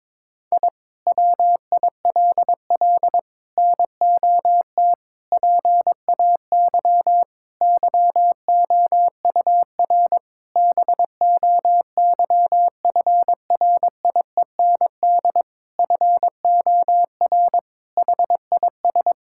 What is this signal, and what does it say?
Morse code, heard as a keyed tone, about 22 wpm, 700 Hz. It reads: I WILL NOT PAY YOUR BOYFRIEND FOR HIS
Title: Sentences from Top 1000 Words - Review Part 5 - 22wpm